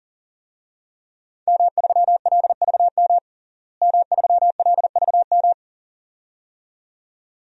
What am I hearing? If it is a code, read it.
Morse code, heard as a keyed tone, about 40 wpm, 700 Hz. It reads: M3LVM M3LVM
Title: DXCC-100 Call Signs - 5 Characters - Repeated 2x - 40wpm